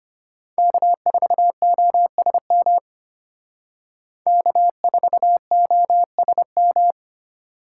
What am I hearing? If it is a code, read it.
Morse code, heard as a keyed tone, about 30 wpm, 700 Hz. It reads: K4OSM K4OSM
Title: Speed Racing - USA 1x3 Call Signs 25wpm